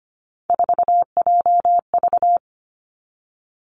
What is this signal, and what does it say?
Morse code, heard as a keyed tone, about 25 wpm, 700 Hz. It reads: 4JV
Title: Three Challenging Letter-Number Combinations 25wpm